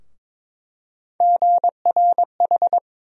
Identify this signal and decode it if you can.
Morse code, heard as a keyed tone, about 22 wpm, 700 Hz. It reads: GRH
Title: USA Call Sign Postfixes - 3 Letters 22wpm